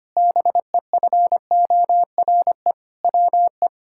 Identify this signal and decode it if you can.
Morse code, heard as a keyed tone, about 25 wpm, 700 Hz. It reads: BEFORE WE
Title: Sentences from Top 1000 Words - Review Part 6 - 25wpm